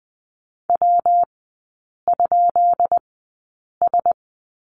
Morse code, heard as a keyed tone, about 20 wpm, spaced as for 8 wpm, 700 Hz. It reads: W?S
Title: 133 Lesson 22e - Question Mark - Three Characters 20wpm